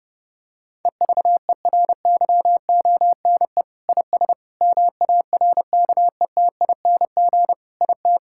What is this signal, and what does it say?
Morse code, heard as a keyed tone, about 30 wpm, 700 Hz. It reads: EVERYONE IS MARKETING IT
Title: Sentences from Top 1000 Words - Review Part 5 - 30wpm